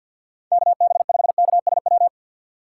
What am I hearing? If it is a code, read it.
Morse code, heard as a keyed tone, about 52 wpm, 700 Hz. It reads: KB5KSW
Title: USA Call Signs - 2x Character Spacing - 50wpm